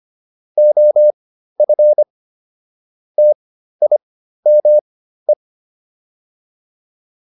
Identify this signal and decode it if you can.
Morse code, heard as a keyed tone, about 25 wpm, spaced as for 13 wpm, 600 Hz. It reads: OF TIME
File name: WD302 13